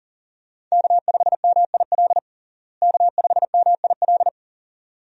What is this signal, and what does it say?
Morse code, heard as a keyed tone, about 37 wpm, 700 Hz. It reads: K5MIL K5MIL
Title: USA Call Signs - 5 Characters - Repeated 2x - 40wpm